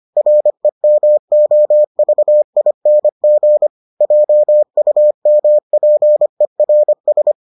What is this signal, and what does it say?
Morse code, heard as a keyed tone, about 25 wpm, 600 Hz. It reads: REMOVING JUMPERS